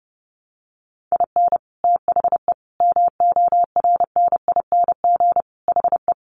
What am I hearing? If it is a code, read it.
Morse code, heard as a keyed tone, about 30 wpm, 700 Hz. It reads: IN THE MORNING HE
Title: Sentences from Top 1200 Words 30wpm